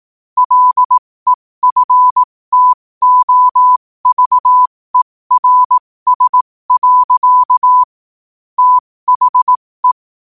Morse code, heard as a keyed tone, about 18 wpm, spaced as for 15 wpm, 1000 Hz. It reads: LEFTOVERS. THE
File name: 15wpm